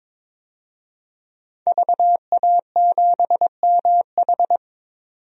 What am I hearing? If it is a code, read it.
Morse code, heard as a keyed tone, about 22 wpm, 700 Hz. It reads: VA7MH
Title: DXCC-100 Call Signs - 5 Characters - 22wpm